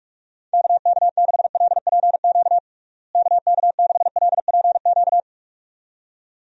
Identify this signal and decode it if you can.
Morse code, heard as a keyed tone, about 46 wpm, 700 Hz. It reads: KK6LPQ KK6LPQ
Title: USA Call Signs - 6 Characters - Repeated 2x - 45wpm